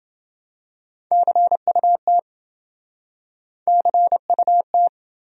Morse code, heard as a keyed tone, about 29 wpm, 700 Hz. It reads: CUT CUT